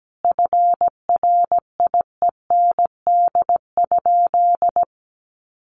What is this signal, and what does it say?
Morse code, heard as a keyed tone, about 17 wpm, 700 Hz. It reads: FRIEND?
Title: Sentences from Top 1400 Words 17wpm